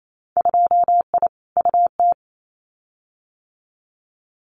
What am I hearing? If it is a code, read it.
Morse code, heard as a keyed tone, about 28 wpm, 700 Hz. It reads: 2I UT
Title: Winter Field Day Exchanges 28wpm